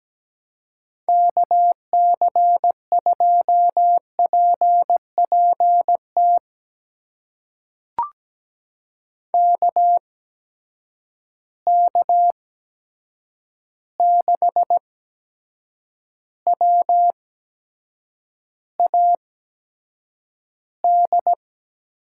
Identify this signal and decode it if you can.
Morse code, heard as a keyed tone, about 17 wpm, spaced as for 7 wpm, 700 Hz. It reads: KC2PPT KK6WAD
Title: USA Call Signs - 6 Characters - 8x Character Spacing - 17wpm